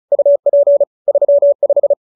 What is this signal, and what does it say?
Morse code, heard as a keyed tone, about 34 wpm, 570 Hz. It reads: UP 35